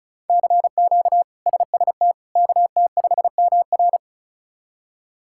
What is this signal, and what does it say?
Morse code, heard as a keyed tone, about 34 wpm, 700 Hz. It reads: CQ SST KT5MR
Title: K1USN SST QSO Practice - 35wpm